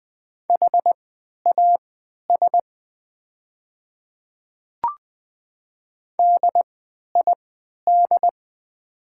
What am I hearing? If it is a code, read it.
Morse code, heard as a keyed tone, about 20 wpm, spaced as for 11 wpm, 700 Hz. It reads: HAS DID
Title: 3x Character Spacing - Top 100 Words - 20wpm